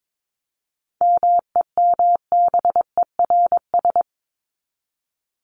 Morse code, heard as a keyed tone, about 22 wpm, 700 Hz. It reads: MEMBERS
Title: ES:S Suffix - Top 100 Words with Base Words 22wpm